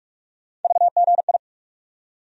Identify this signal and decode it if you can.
Morse code, heard as a keyed tone, about 46 wpm, 700 Hz. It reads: VGI
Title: USA Call Sign Postfixes - 3 Letters 45wpm